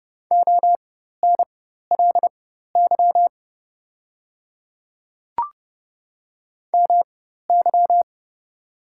Morse code, heard as a keyed tone, about 30 wpm, spaced as for 14 wpm, 700 Hz. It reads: ONLY MY
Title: 4x Character Spacing - Top 200 Words - 30wpm